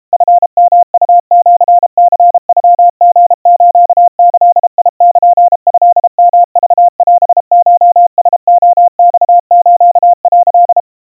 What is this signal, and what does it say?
Morse code (Japanese Wabun code), heard as a keyed tone, about 32 wpm, 700 Hz. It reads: チヨウシニノリスギルトヨクオコラレマス」
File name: kawa / Morse sample / kamyu 120 cpm 700 Hz Crispy